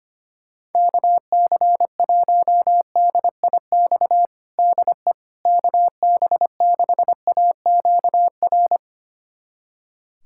Morse code, heard as a keyed tone, about 25 wpm, 700 Hz. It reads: KC1DIX DE KB6AQR
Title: USA Callsign DE Callsign 25wpm